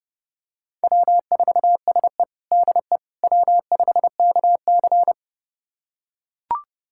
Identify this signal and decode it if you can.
Morse code, heard as a keyed tone, about 30 wpm, 700 Hz. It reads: W4SE DE W5KC
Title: USA Callsign DE Callsign - 4 Characters - 30wpm